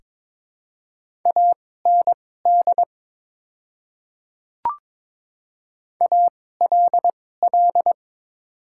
Morse code, heard as a keyed tone, about 22 wpm, spaced as for 16 wpm, 700 Hz. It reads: AND ALL
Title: 2x Character Spacing - Top 100 Words - 22wpm